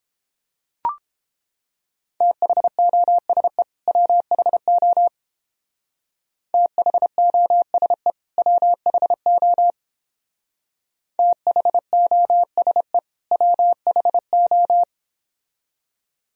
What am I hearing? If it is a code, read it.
Morse code, heard as a keyed tone, about 29 wpm, 700 Hz. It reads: THOSE WHO THOSE WHO THOSE WHO